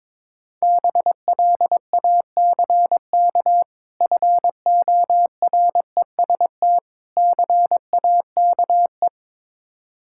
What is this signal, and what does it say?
Morse code, heard as a keyed tone, about 22 wpm, 690 Hz. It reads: BLACK FOREST CAKE
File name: Difficult Words with Hints - 22wpm